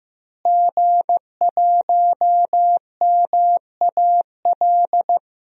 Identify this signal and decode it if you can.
Morse code, heard as a keyed tone, about 15 wpm, 700 Hz. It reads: G1MAL
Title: DXCC-100 Call Signs - 5 Characters - Repeated 3x - 15wpm